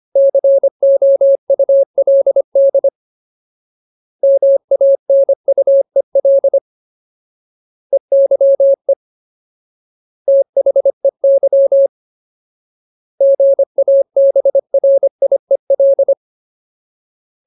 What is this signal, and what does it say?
Morse code, heard as a keyed tone, about 25 wpm, 550 Hz. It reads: COULD MANUEL EYE THEY GABRIEL
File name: Ultimate Word List / 25WPM Ultimate